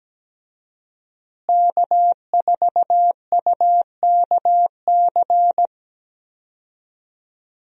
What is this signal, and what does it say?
Morse code, heard as a keyed tone, about 17 wpm, 700 Hz. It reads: K4UKC
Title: USA Call Signs - 5 Characters - Repeated 3x - 17wpm